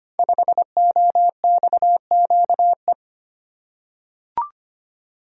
Morse code, heard as a keyed tone, about 25 wpm, 700 Hz. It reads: 5OXQE
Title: Code Groups - 5 Letter:Numbers - 25wpm